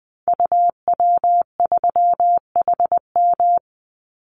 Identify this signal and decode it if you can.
Morse code, heard as a keyed tone, about 20 wpm, 700 Hz. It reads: UW3HM
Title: Call Signs - 20wpm